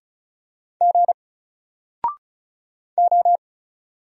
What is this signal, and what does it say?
Morse code, heard as a keyed tone, about 36 wpm, 700 Hz. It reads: G O
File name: Single-Letter-Number-35wpm